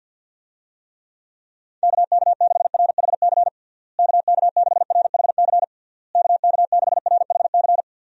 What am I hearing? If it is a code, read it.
Morse code, heard as a keyed tone, about 51 wpm, 700 Hz. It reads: KK6RHC KK6RHC KK6RHC
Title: USA Call Signs - 6 Characters - Repeated 3x - 50wpm